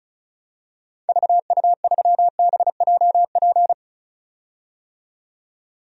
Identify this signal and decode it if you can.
Morse code, heard as a keyed tone, about 36 wpm, 700 Hz. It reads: VU3BJP
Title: DXCC-100 Call Signs - 6 Characters - 35wpm